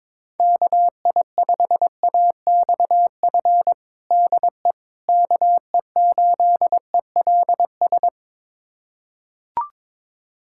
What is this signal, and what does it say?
Morse code, heard as a keyed tone, about 22 wpm, 700 Hz. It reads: KI5AXF DE KE8ELS
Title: USA Callsign DE Callsign - 6 Characters - 22wpm